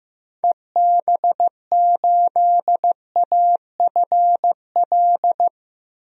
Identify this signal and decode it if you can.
Morse code, heard as a keyed tone, about 15 wpm, 700 Hz. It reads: EB8AFL
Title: DXCC-100 Call Signs - 6 Characters - Repeated 2x - 15wpm